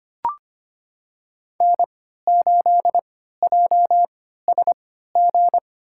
Morse code, heard as a keyed tone, about 25 wpm, spaced as for 14 wpm, 700 Hz. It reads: N8JSG
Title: USA Call Signs - 5 Characters - 3x Character Spacing - 25wpm